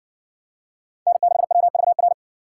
Morse code, heard as a keyed tone, about 60 wpm, 700 Hz. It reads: N6WVR